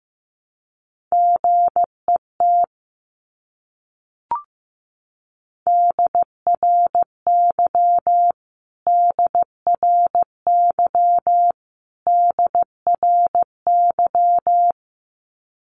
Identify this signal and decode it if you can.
Morse code, heard as a keyed tone, about 14 wpm, 700 Hz. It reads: GET DRY DRY DRY
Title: Top 500 Words Repeated 15wpm